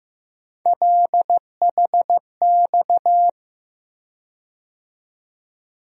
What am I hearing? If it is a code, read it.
Morse code, heard as a keyed tone, about 15 wpm, 700 Hz. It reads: LHX
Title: USA Call Sign Postfixes - 3 Letters 15wpm